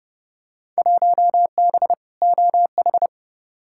Morse code, heard as a keyed tone, about 30 wpm, 700 Hz. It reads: 1B OH
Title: Field Day Exchanges - 30wpm